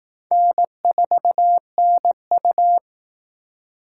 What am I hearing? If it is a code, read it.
Morse code, heard as a keyed tone, about 18 wpm, 700 Hz. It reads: N4NU
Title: Speed Racing - USA 1x2 Call Signs 15wpm